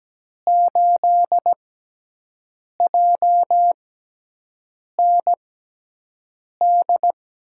Morse code, heard as a keyed tone, about 17 wpm, spaced as for 6 wpm, 700 Hz. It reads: 8JND
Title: USA Call Signs - 5 Characters - 6x Character Spacing - 17wpm